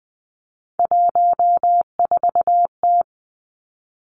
Morse code, heard as a keyed tone, about 20 wpm, 700 Hz. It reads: 14T
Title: Three Challenging Letter-Number Combinations 20wpm